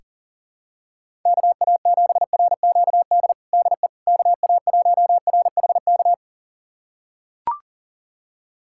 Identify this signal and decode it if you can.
Morse code, heard as a keyed tone, about 37 wpm, 700 Hz. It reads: KA7RQD DE KA1RHK
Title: USA Callsign DE Callsign - 6 Characters - 40wpm